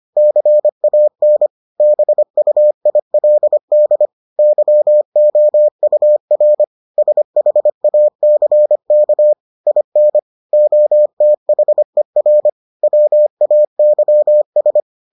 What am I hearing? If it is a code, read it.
Morse code, heard as a keyed tone, about 25 wpm, 600 Hz. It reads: CAN BUILD YOUR SHACK IN OTHER WAYS